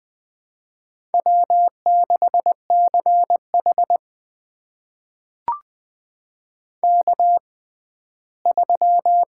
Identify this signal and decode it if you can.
Morse code, heard as a keyed tone, about 21 wpm, spaced as for 14 wpm, 700 Hz. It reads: W6CH K3
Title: USA Call Signs - 4 Characters - 6x Character Spacing - 20wpm